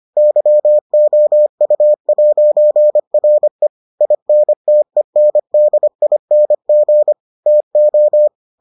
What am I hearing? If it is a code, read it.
Morse code, heard as a keyed tone, about 25 wpm, 600 Hz. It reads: YOU'RE INTENDING TO